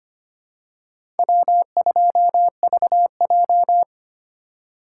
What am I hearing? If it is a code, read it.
Morse code, heard as a keyed tone, about 25 wpm, 700 Hz. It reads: W2VJ
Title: USA Call Signs - 4 Characters - 3x Character Spacing - 25wpm